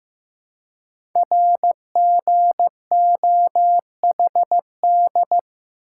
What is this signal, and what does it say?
Morse code, heard as a keyed tone, about 15 wpm, 700 Hz. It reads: RGOHD
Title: Code Groups - 5 Letters - 15wpm